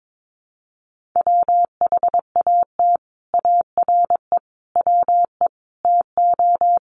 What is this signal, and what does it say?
Morse code, heard as a keyed tone, about 22 wpm, 700 Hz. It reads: WHAT ARE WE TO
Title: Sentences from Top 100 Words 22wpm